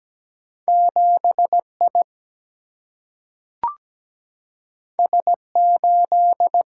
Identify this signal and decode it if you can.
Morse code, heard as a keyed tone, about 17 wpm, 700 Hz. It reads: 7I S8
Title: Single-Letter Single-Number 17wpm